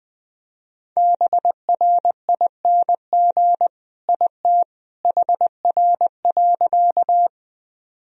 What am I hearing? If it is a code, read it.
Morse code, heard as a keyed tone, about 21 wpm, 700 Hz. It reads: BRING IT HR.
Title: Sentences from Top 500 Words and Abbreviations 20wpm